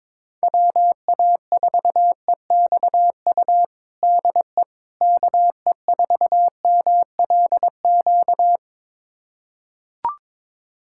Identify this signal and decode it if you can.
Morse code, heard as a keyed tone, about 22 wpm, 700 Hz. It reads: WA4EXU DE KE4MLQ
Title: USA Callsign DE Callsign - 6 Characters - 22wpm